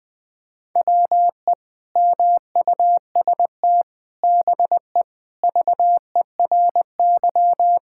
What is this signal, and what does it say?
Morse code, heard as a keyed tone, about 20 wpm, 700 Hz. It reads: WE MUST BE VERY
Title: Sentences from Top 1600 Words 20wpm